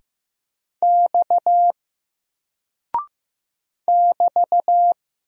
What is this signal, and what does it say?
Morse code, heard as a keyed tone, about 15 wpm, 700 Hz. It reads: X <BT>
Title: Single Character (Letter:Number:Punctuation:Prosign) 15wpm